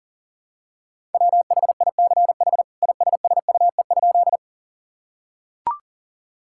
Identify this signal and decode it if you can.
Morse code, heard as a keyed tone, about 40 wpm, 700 Hz. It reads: WHICH ISSUE?
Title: Sentences from Top 1000 Words - Review Part 6 - 40wpm